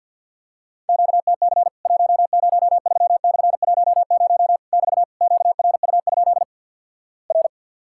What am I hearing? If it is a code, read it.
Morse code, heard as a keyed tone, about 50 wpm, 700 Hz. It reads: QTC 103/10 <BT> QRU? R
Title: Worked All Europe DX Contest - QTC Exchange - 50wpm